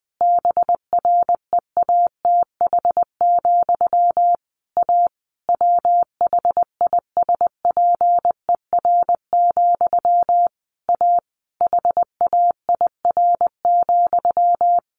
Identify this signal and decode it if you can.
Morse code, heard as a keyed tone, about 20 wpm, 700 Hz. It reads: BREATH, A WHISPER, A HAIR,
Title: A Sound Of Thunder 20wpm